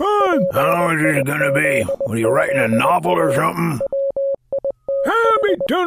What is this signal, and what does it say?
Morse code, heard as a keyed tone, about 20 wpm, 550 Hz, with speech. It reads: PIZZA WITH